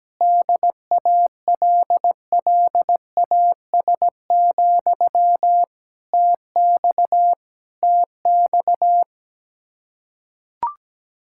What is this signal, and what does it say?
Morse code, heard as a keyed tone, about 17 wpm, 700 Hz. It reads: DALLAS, TX TX
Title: Practice QSO - Exchange 1 - Beginning - 17wpm